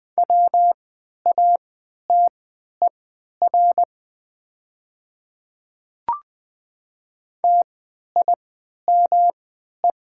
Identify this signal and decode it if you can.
Morse code, heard as a keyed tone, about 20 wpm, spaced as for 11 wpm, 700 Hz. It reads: WATER TIME
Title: 3x Character Spacing - Top 100 Words - 20wpm